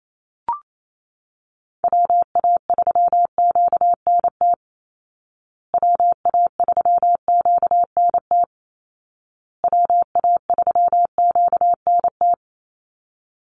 Morse code, heard as a keyed tone, about 28 wpm, 700 Hz. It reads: WA3QNT WA3QNT WA3QNT
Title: Call Signs Repeated 2x - 28wpm